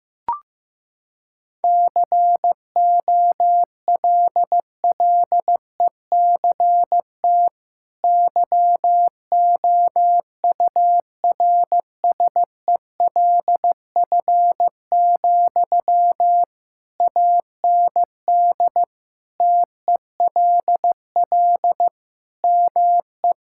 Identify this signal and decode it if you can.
Morse code, heard as a keyed tone, about 15 wpm, 700 Hz. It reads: COLLECT YOURSELF, AND TELL ME
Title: Sentences from Top 1600 Words 15wpm